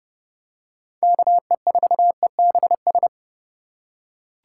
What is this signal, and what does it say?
Morse code, heard as a keyed tone, about 29 wpm, 700 Hz. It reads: KE4EBS